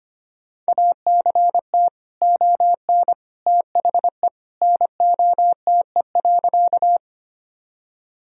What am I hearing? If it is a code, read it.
Morse code, heard as a keyed tone, about 25 wpm, 700 Hz. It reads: ACT ON THE NOTE.
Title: Sentences from Top 1000 Words - Review Part 2 - 25wpm